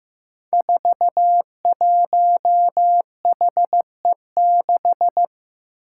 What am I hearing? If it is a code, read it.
Morse code, heard as a keyed tone, about 15 wpm, 700 Hz. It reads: 41HE6
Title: Five Challenging Letter-Number Combinations 15wpm